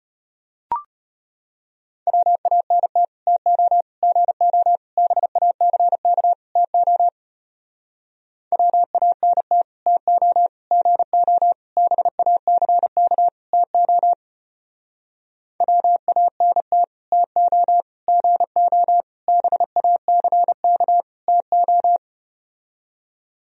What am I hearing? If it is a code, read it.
Morse code, heard as a keyed tone, about 34 wpm, 700 Hz. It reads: WANT TO GO BACK TO WANT TO GO BACK TO WANT TO GO BACK TO